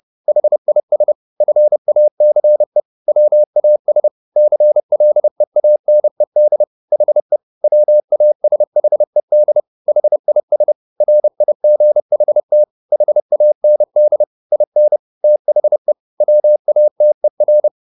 Morse code, heard as a keyed tone, about 30 wpm, 600 Hz. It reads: HIS FACE WAS CLEANED HE WASHED HIS RIGHT HAND IN THE WATER